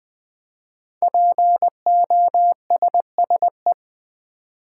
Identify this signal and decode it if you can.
Morse code, heard as a keyed tone, about 20 wpm, 700 Hz. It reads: POSSE
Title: Build A Word - 5 Letters At A Time - 10-Letter Words - 20wpm